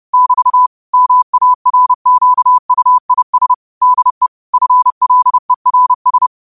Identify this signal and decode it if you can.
Morse code, heard as a keyed tone, about 30 wpm, 1000 Hz. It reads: <BT> MARQUIS DE FLERS
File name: qotdfiles / 30wpm